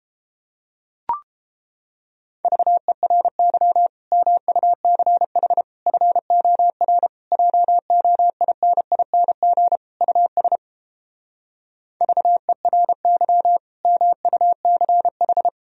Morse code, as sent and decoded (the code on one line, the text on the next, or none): ...- . .-. -.-- / -- ..- -.-. .... / ..-. --- .-. / .--- --- .. -. .. -. --. / ..- ... / ...- . .-. -.-- / -- ..- -.-. ....
VERY MUCH FOR JOINING US VERY MUCH